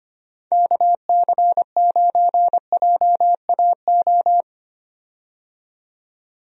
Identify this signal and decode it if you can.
Morse code, heard as a keyed tone, about 25 wpm, 700 Hz. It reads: KC9JAO
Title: USA Call Signs - 6 Characters - 25wpm